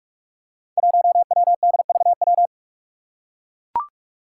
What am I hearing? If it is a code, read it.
Morse code, heard as a keyed tone, about 43 wpm, 700 Hz. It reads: 1WDVW
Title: Five Challenging Letter-Number Combinations 45wpm